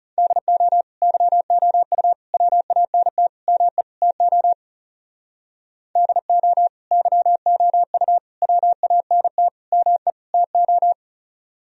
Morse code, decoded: DO YOU WANT ME TO DO YOU WANT ME TO

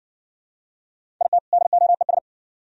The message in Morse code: IT DOES

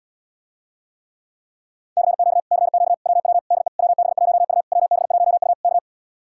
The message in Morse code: CQ CQ WW DE WB1U WB1U K

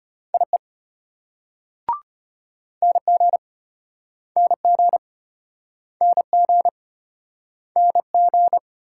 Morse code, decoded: IE NG NG NG NG

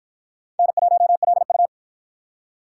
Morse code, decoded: N1LU

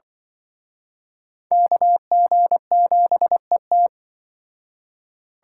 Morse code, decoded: KG7ET

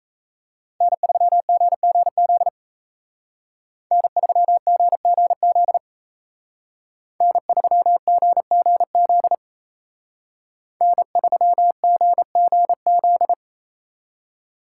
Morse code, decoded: N3GGZ N3GGZ N3GGZ N3GGZ

